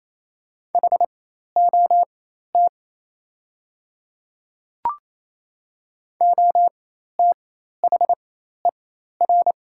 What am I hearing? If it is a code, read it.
Morse code, heard as a keyed tone, about 28 wpm, spaced as for 13 wpm, 700 Hz. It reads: HOT OTHER
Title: 4x Character Spacing - Top 100 Words - 28wpm